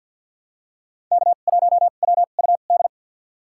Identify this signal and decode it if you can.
Morse code, heard as a keyed tone, about 47 wpm, spaced as for 35 wpm, 700 Hz. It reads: K1WUD
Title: USA Call Signs - 2x Character Spacing - 50wpm